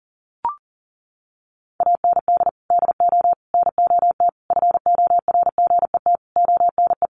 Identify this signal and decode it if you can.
Morse code, heard as a keyed tone, about 37 wpm, 700 Hz. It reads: AND DO NOT FORGET ONE
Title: Sentences from Top 1200 Words 40wpm